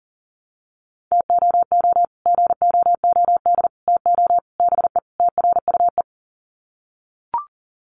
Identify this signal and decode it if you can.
Morse code, heard as a keyed tone, about 40 wpm, 700 Hz. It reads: TOO GOOD TO BE TRUE
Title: Sets of 5 Words Encore 40wpm